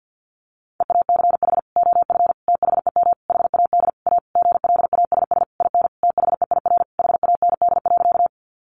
Morse code, heard as a keyed tone, about 50 wpm, 700 Hz. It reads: EACH OF THEM HAD A GLASS IN THEIR HAND.
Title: Sentences from Top 1300 Words 50wpm